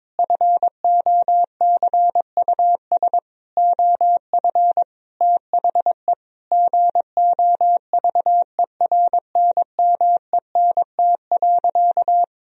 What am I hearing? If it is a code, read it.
Morse code, heard as a keyed tone, about 22 wpm, 700 Hz. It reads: FOCUS OF THE GOVERNMENT.
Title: Sentences from Top 1000 Words - Review Part 4 - 22wpm